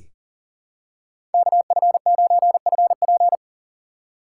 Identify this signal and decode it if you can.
Morse code, heard as a keyed tone, about 38 wpm, 700 Hz. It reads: KF9FP
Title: USA Call Signs - 5 Characters - 6x Character Spacing - 40wpm